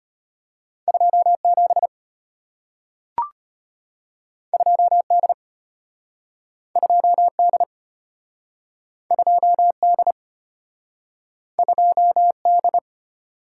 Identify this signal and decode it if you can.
Morse code, heard as a keyed tone, about 34 wpm, 700 Hz. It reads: 27 2D 2D 2D 2D